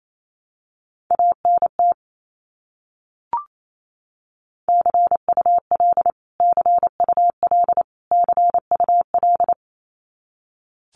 Morse code, heard as a keyed tone, about 28 wpm, 700 Hz. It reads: ANT CUL CUL CUL
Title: Abbreviations Repeated 3x - 28wpm